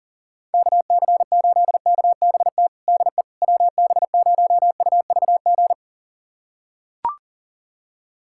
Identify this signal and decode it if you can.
Morse code, heard as a keyed tone, about 40 wpm, 700 Hz. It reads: KC8KBT DE WB0UVG
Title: USA Callsign DE Callsign - 6 Characters - 40wpm